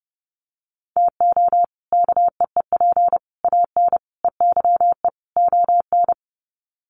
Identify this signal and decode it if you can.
Morse code, heard as a keyed tone, about 30 wpm, 700 Hz. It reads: TO KEEP AN EYE ON
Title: Sets of 5 Words Encore 30wpm